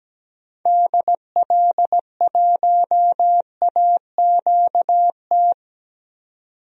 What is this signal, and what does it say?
Morse code, heard as a keyed tone, about 17 wpm, 700 Hz. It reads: DL1AQT
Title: DXCC-100 Call Signs - 6 Characters - 17wpm